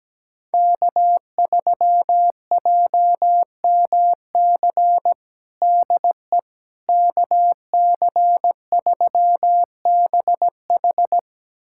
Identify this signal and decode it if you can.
Morse code, heard as a keyed tone, about 17 wpm, 700 Hz. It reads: K3JMC DE KC3BH
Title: USA Callsign DE Callsign - 5 Characters - 17wpm